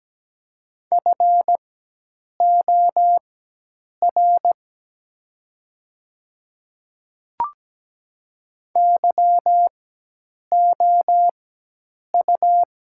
Morse code, heard as a keyed tone, about 17 wpm, spaced as for 8 wpm, 700 Hz. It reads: FOR YOU
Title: 4x Character Spacing - Top 100 Words - 17wpm